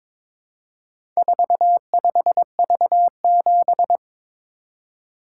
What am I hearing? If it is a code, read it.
Morse code, heard as a keyed tone, about 22 wpm, 700 Hz. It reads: 45V7
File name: practice / 22wpm / Four Letter-Number Combinations - 22wpm